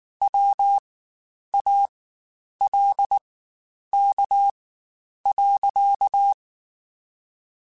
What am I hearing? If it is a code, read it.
Morse code, heard as a keyed tone, about 19 wpm, spaced as for 9 wpm, 800 Hz. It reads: WALK.
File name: Quotes Spaced / Quotes Spaced-mp3s / Temper sp @18